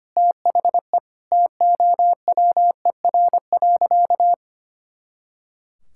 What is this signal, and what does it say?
Morse code, heard as a keyed tone, about 25 wpm, 700 Hz. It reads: THE TOWER.
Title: Pangram Challenge v2 - 25wpm